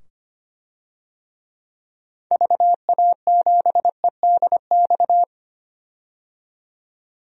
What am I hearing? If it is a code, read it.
Morse code, heard as a keyed tone, about 25 wpm, 700 Hz. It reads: VA7EDX
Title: DXCC-100 Call Signs - 6 Characters - 25wpm